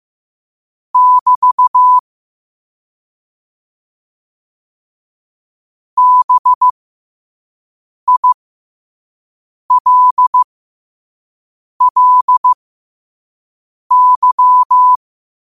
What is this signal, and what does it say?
Morse code, heard as a keyed tone, about 15 wpm, spaced as for 5 wpm, 1000 Hz. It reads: <BT> BILLY